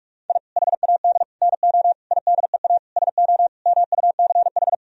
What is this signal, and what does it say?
Morse code, heard as a keyed tone, about 46 wpm, 700 Hz. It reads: I HAD NO IDEA SO MUCH
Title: Sentences from Top 1000 Words - Review Part 4 - 45wpm